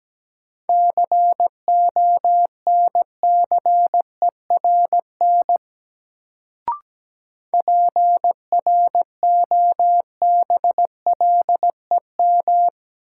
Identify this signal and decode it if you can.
Morse code, heard as a keyed tone, about 17 wpm, 700 Hz. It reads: CONCERN PROBLEM
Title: Sentences from Top 1000 Words - New Word Drill - 17wpm